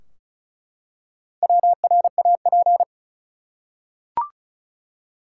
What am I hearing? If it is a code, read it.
Morse code, heard as a keyed tone, about 34 wpm, 700 Hz. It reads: WRAP
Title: Four Letter Words - 35wpm